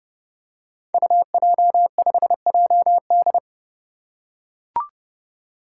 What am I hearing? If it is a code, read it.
Morse code, heard as a keyed tone, about 30 wpm, 700 Hz. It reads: UJ5JD
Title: Code Groups - 5 Letter:Numbers - 30wpm